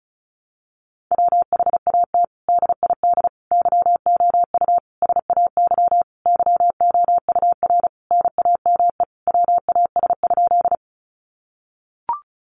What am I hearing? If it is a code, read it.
Morse code, heard as a keyed tone, about 34 wpm, 700 Hz. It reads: WHAT DID YOU SAY YOUR NAME WAS?
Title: Sentences from Top 200 Words 35wpm